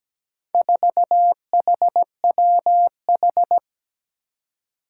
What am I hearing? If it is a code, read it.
Morse code, heard as a keyed tone, about 17 wpm, 700 Hz. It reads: 4HWH
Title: Four Challenging Letter-Number Combinations 17wpm